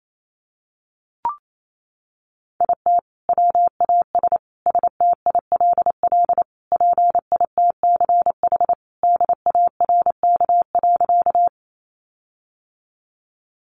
Morse code, heard as a keyed tone, about 28 wpm, 700 Hz. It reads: IT WAS STILL PITCH DARK.
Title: Sentences from Top 1300 Words 28wpm